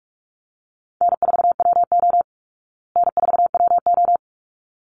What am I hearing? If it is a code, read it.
Morse code, heard as a keyed tone, about 44 wpm, 700 Hz. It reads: N4WO N4WO
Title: Call Signs Repeated 3x - 45wpm